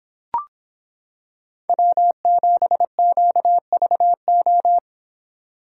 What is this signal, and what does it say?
Morse code, heard as a keyed tone, about 26 wpm, 700 Hz. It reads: W7QVO